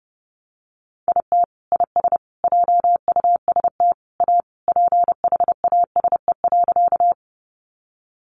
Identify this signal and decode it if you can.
Morse code, heard as a keyed tone, about 30 wpm, 700 Hz. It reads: IT IS JUST A PHASE.
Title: Sentences from Top 1200 Words 30wpm